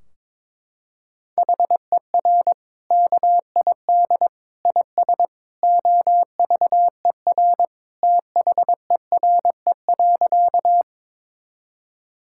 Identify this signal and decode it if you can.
Morse code, heard as a keyed tone, about 22 wpm, 700 Hz. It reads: HER KID IS OVER THERE.
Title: Sentences from Top 1600 Words 22wpm